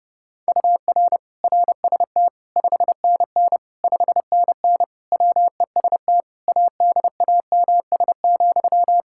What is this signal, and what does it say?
Morse code, heard as a keyed tone, about 30 wpm, 700 Hz. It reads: UR RST 5NN 5NN WEST ADAMS,
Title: Practice QSO - Exchange 1 - Beginning - 30wpm